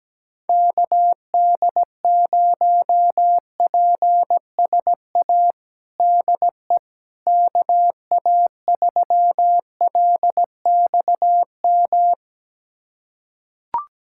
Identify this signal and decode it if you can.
Morse code, heard as a keyed tone, about 17 wpm, 700 Hz. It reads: KD0PSA DE KA3LXM
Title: USA Callsign DE Callsign - 6 Characters - 17wpm